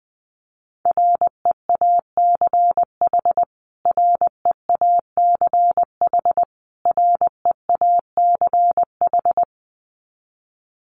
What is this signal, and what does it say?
Morse code, heard as a keyed tone, about 20 wpm, 700 Hz. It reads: REACH REACH REACH
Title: Top 500 Words Repeated 20wpm